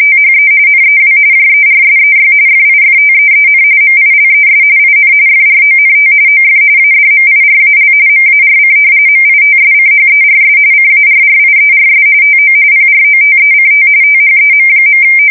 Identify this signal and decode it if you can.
RTTY, about 45.45 baud. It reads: THE NEW JERSEY FIFTEEN MILES IN FRONT OF ADMIRAL MARC MITSCHERS CARRIERS TO PROTECT THEM.
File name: fsk170 / battleshipnj